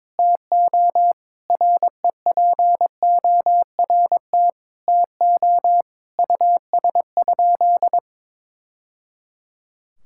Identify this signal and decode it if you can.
Morse code, heard as a keyed tone, about 22 wpm, 700 Hz. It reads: TO REPORT TO US?
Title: Sentences from Top 1000 Words - Review Part 6 - 22wpm